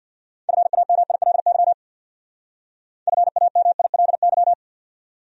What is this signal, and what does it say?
Morse code, heard as a keyed tone, about 65 wpm, 700 Hz. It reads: FAMILY FAMILY